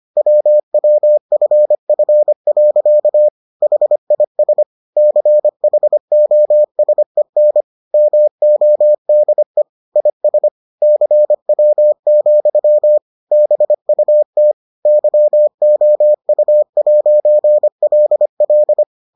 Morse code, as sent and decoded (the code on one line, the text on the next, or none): .-- .-- ..-. ..-. .-.-.- / .... .. ... / -.-. .... --- ... -. / -- --- -.. / .. ... / -.-. .-- --..-- / -... ..- - / -.-- --- ..- .----. .-.. .-..
WWFF. HIS CHOSEN MODE IS CW, BUT YOU'LL